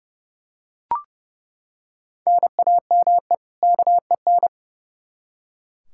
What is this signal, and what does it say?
Morse code, heard as a keyed tone, about 30 wpm, 700 Hz. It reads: NAME KEN
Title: K1USN SST QSO Practice 30wpm